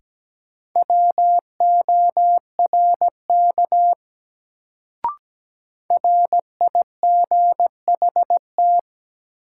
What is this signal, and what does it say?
Morse code, heard as a keyed tone, about 17 wpm, 700 Hz. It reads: WORK RIGHT
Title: Sets of 2 Words - New Word Drill - 17wpm